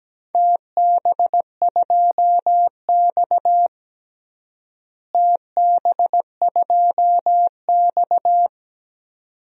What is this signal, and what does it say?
Morse code, heard as a keyed tone, about 17 wpm, 700 Hz. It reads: TB2X TB2X
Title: DXCC-100 Call Signs - 4 Characters - Repeated 2x - 17wpm